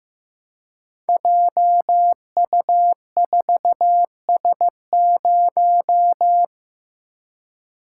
Morse code, heard as a keyed tone, about 15 wpm, 700 Hz. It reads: JU4S0
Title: Code Groups - 5 Letter:Numbers - 15wpm